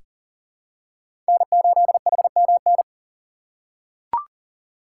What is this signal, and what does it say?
Morse code, heard as a keyed tone, about 40 wpm, 700 Hz. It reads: N8HMN
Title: USA Call Signs - 5 Characters - 3x Character Spacing - 40wpm